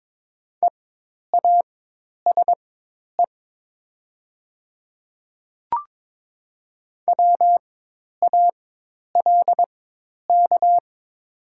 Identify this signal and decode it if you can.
Morse code, heard as a keyed tone, about 22 wpm, spaced as for 10 wpm, 700 Hz. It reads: EASE WALK
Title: 4x Character Spacing - Top 300 Words - 22wpm